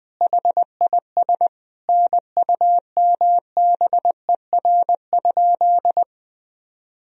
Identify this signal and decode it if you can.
Morse code, heard as a keyed tone, about 21 wpm, 700 Hz. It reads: HIS NUMBER?
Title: Sentences from Top 1000 Words - Review Part 1 - 20wpm